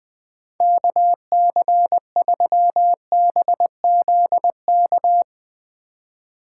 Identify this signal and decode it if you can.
Morse code, heard as a keyed tone, about 23 wpm, 700 Hz. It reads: KC3BZK
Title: Speed Racing - USA 2x3 Call Signs 20wpm